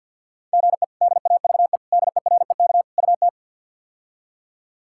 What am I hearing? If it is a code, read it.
Morse code, heard as a keyed tone, about 53 wpm, 690 Hz. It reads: GE DAVE DEREK UT
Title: K1USN SST QSO Practice - 50wpm